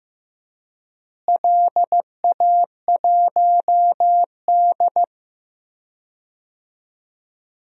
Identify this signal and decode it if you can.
Morse code, heard as a keyed tone, about 15 wpm, 700 Hz. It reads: LA1D
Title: DXCC-100 Call Signs - 4 Characters - Repeated 3x - 15wpm